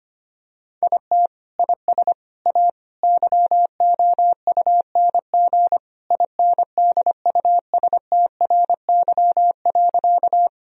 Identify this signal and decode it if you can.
Morse code, heard as a keyed tone, about 25 wpm, 700 Hz. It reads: IT IS A YOUNG INDUSTRY.
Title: Sentences from Top 1000 Words - Review Part 4 - 25wpm